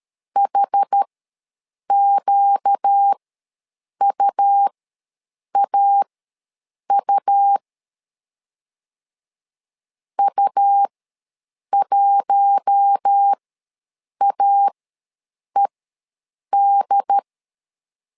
Morse code, heard as a keyed tone, about 13 wpm, spaced as for 7 wpm, 800 Hz. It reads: HQUAU U1AED
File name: LEZIONI / 4 Q1 ws 12wpm 2129